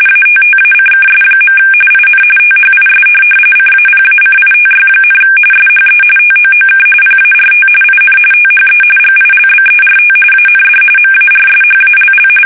RTTY, about 45.45 baud. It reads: ATTACKS. IN THIS CAPACITY, SHE TOOK PART IN THE GILBERT AND MARSHALL
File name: fsk850 / USS-Massachusetts